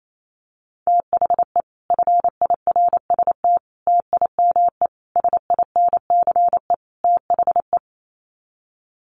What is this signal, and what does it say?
Morse code, heard as a keyed tone, about 28 wpm, 700 Hz. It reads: THE FIRST TIME SINCE THE
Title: Sets of 5 Words Encore 28wpm